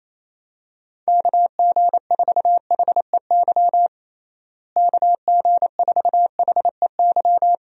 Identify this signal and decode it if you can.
Morse code, heard as a keyed tone, about 28 wpm, 700 Hz. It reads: KG4HEY KG4HEY
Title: USA Call Signs - 6 Characters - Repeated 3x - 28wpm